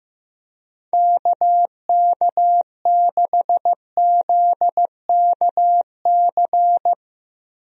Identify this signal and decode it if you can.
Morse code, heard as a keyed tone, about 15 wpm, 700 Hz. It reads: KK6ZKC
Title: USA Call Signs - 6 Characters - Repeated 3x - 15wpm